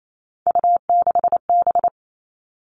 Morse code, heard as a keyed tone, about 28 wpm, 700 Hz. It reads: U6B
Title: All Cut Numbers - 3-Digit 28wpm